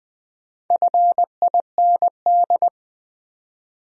Morse code, heard as a keyed tone, about 20 wpm, 700 Hz. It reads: FIND
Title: Top 100 Words 20wpm